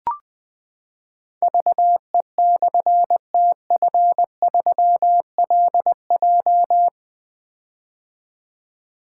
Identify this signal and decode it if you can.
Morse code, heard as a keyed tone, about 20 wpm, 700 Hz. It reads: VE/TF3LJ
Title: Difficult Call Signs 20wpm